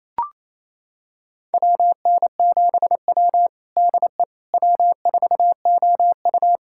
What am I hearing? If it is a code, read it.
Morse code, heard as a keyed tone, about 27 wpm, 700 Hz. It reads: WN7W DE W4OU
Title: USA Callsign DE Callsign - 4 Characters - 28wpm